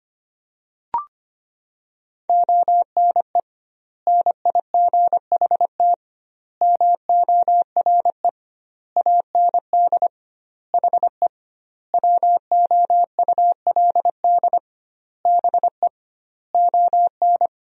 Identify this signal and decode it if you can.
Morse code, heard as a keyed tone, about 25 wpm, 700 Hz. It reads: ONE NIGHT MORE AND HE WOULD BE ON
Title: 2x Word-Spacing - Sentences from Top 300 Words - 25wpm